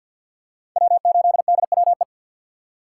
Morse code, heard as a keyed tone, about 51 wpm, 700 Hz. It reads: W8DWE
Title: USA Call Signs - 5 Characters - 4x Character Spacing - 50wpm